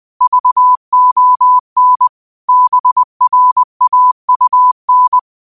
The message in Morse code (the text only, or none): VON BRAUN